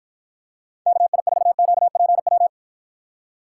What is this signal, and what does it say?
Morse code, heard as a keyed tone, about 53 wpm, 700 Hz. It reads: KI4QPW